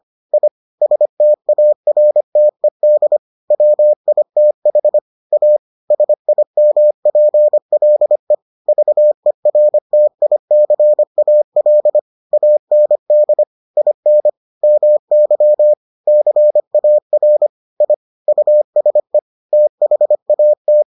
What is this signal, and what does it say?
Morse code, heard as a keyed tone, about 25 wpm, 600 Hz. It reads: I STARTED WITH A SIMPLE VERTICAL AND IN MY CAR I USE THAT